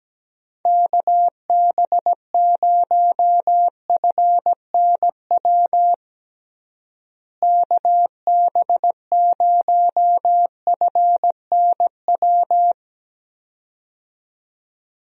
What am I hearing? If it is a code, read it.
Morse code, heard as a keyed tone, about 17 wpm, 700 Hz. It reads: KB0FNW KB0FNW
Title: USA Call Signs - 6 Characters - Repeated 2x - 17wpm